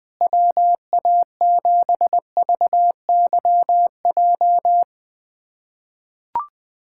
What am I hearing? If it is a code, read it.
Morse code, heard as a keyed tone, about 20 wpm, 700 Hz. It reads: WA7VYJ
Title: USA Callsign DE Callsign 20wpm